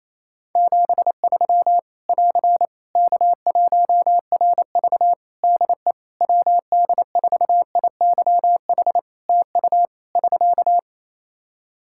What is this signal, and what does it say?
Morse code, heard as a keyed tone, about 27 wpm, 700 Hz. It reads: 73 <AR> K1RV DE WD4IYH TU <SK>
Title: Practice QSO - Exchange 4 - Ending - 28wpm